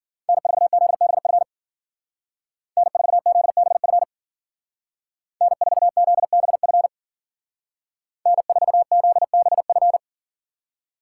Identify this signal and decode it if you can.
Morse code, heard as a keyed tone, about 53 wpm, 700 Hz. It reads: N4ZBF N4ZBF N4ZBF N4ZBF